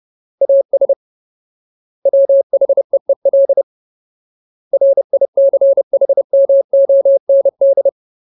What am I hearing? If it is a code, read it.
Morse code, heard as a keyed tone, about 30 wpm, 550 Hz. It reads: AS WHEEL RICHMOND
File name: Ultimate Word List / 30WPM Ultimate